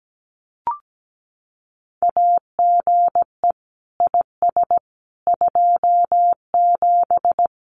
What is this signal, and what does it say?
Morse code, heard as a keyed tone, about 17 wpm, 700 Hz. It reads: AGE IS 27
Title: Common QSO Phrases Encore 17wpm